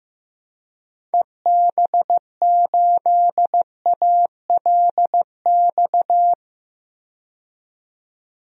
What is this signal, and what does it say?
Morse code, heard as a keyed tone, about 15 wpm, 700 Hz. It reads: EB8ALX
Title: DXCC-100 Call Signs - 6 Characters - 15wpm